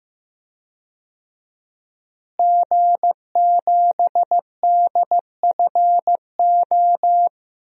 Morse code, heard as a keyed tone, about 15 wpm, 700 Hz. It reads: G7DFO
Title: DXCC-100 Call Signs - 5 Characters - Repeated 2x - 15wpm